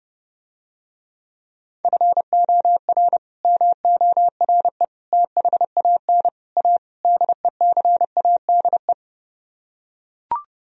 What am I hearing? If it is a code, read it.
Morse code, heard as a keyed tone, about 30 wpm, 700 Hz. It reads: FOR MORE THAN A DECADE